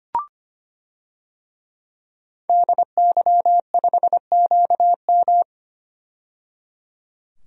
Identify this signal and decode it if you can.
Morse code, heard as a keyed tone, about 25 wpm, 700 Hz. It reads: DY5QM
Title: DXCC-100 Call Signs - 5 Characters - 25wpm